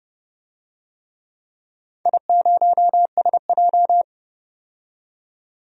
Morse code, heard as a keyed tone, about 29 wpm, 700 Hz. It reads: I0SJ
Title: DXCC-100 Call Signs - 4 Characters - 30wpm